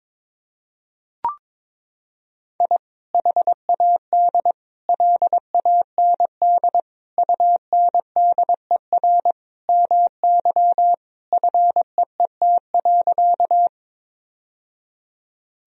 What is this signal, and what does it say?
Morse code, heard as a keyed tone, about 22 wpm, 700 Hz. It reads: I HAD LAND UNDER MY FEET.
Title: Sentences from Top 1000 Words - Review Part 1 - 22wpm